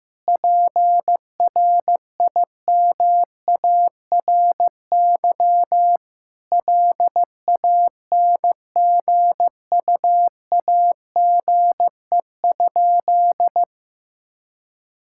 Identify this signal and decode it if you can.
Morse code, heard as a keyed tone, about 15 wpm, 700 Hz. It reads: PRIMARY LANGUAGE?
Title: Sentences from Top 1000 Words - Review Part 5 - 15wpm